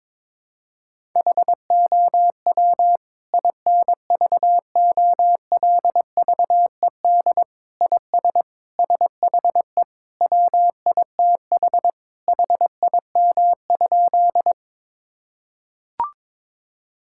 Morse code, heard as a keyed tone, about 22 wpm, 700 Hz. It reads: HOW INVOLVED IS SHE WITH HIM?
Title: Sentences from Top 1000 Words - Review Part 5 - 22wpm